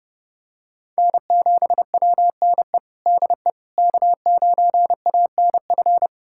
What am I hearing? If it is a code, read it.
Morse code, heard as a keyed tone, about 32 wpm, 700 Hz. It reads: N7WNE DE K9ANF
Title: USA Callsign DE Callsign - 5 Characters - 30wpm